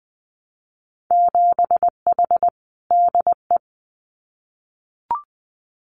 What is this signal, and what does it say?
Morse code, heard as a keyed tone, about 20 wpm, 700 Hz. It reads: 7H DE
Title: Winter Field Day Exchanges Encore 20wpm